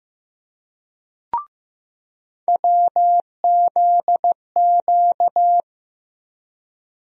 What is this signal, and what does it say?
Morse code, heard as a keyed tone, about 15 wpm, 700 Hz. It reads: WZQ
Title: USA Call Sign Postfixes - 3 Letters 15wpm